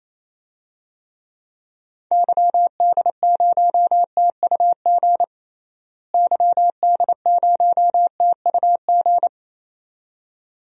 Morse code, heard as a keyed tone, about 28 wpm, 700 Hz. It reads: YD0TUG YD0TUG
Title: DXCC-100 Call Signs - 6 Characters - Repeated 2x - 28wpm